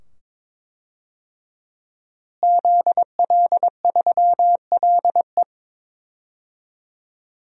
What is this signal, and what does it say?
Morse code, heard as a keyed tone, about 22 wpm, 700 Hz. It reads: ZL3LE
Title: DXCC-100 Call Signs - 5 Characters - Repeated 2x - 22wpm